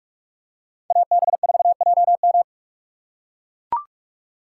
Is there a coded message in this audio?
Yes: Morse code from 0.9 to 2.4 s, heard as a keyed tone, about 43 wpm, 700 Hz. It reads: AB4JM